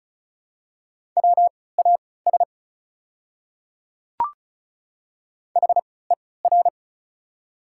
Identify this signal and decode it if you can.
Morse code, heard as a keyed tone, about 36 wpm, spaced as for 20 wpm, 690 Hz. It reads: WAS HER
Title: 3x Character Spacing - Top 100 Words - 35wpm